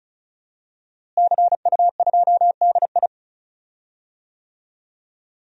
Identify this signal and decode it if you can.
Morse code, heard as a keyed tone, about 34 wpm, 700 Hz. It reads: CU2DI
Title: DXCC-100 Call Signs - 5 Characters - Repeated 2x - 35wpm